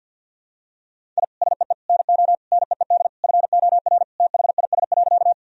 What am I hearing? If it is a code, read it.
Morse code, heard as a keyed tone, about 46 wpm, 700 Hz. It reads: I SEE NO NEED FOR THIS.
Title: Sentences from Top 1000 Words - Review Part 1 - 50wpm